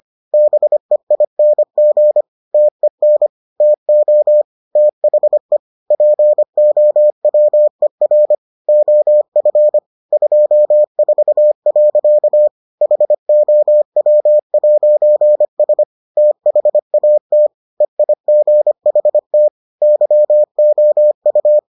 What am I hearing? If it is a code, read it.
Morse code, heard as a keyed tone, about 25 wpm, 600 Hz. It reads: BEING TEN TO THE POWER OF 24. HOW'S THAT EIGHT YOU